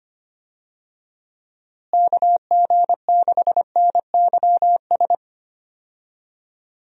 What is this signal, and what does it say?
Morse code, heard as a keyed tone, about 25 wpm, 700 Hz. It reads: KG6NYS
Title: USA Call Signs - 6 Characters - Repeated 3x - 25wpm